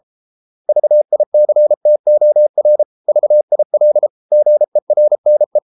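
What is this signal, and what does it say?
Morse code, heard as a keyed tone, about 33 wpm, 600 Hz. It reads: VICTOR VIL GERNE
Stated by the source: Morse code at 33 wpm, QRQ.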